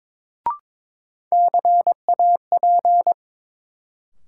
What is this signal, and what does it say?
Morse code, heard as a keyed tone, about 22 wpm, 700 Hz. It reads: CAP
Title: Sentences from Top 1600 Words - New Word Drill Alt - 22wpm